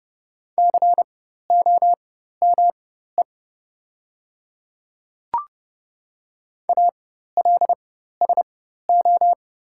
Morse code, heard as a keyed tone, about 30 wpm, spaced as for 14 wpm, 700 Hz. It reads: COME ALSO
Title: 4x Character Spacing - Top 200 Words - 30wpm